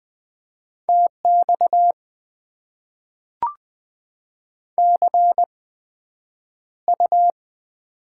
Morse code, heard as a keyed tone, about 20 wpm, spaced as for 9 wpm, 700 Hz. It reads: TX CU
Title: Top 90 QSO Elements - 8x Character Spacing - 20wpm